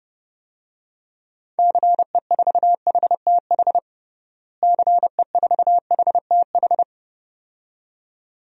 Morse code, heard as a keyed tone, about 30 wpm, 700 Hz. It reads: CE4HTH CE4HTH
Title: DXCC-100 Call Signs - 6 Characters - Repeated 2x - 30wpm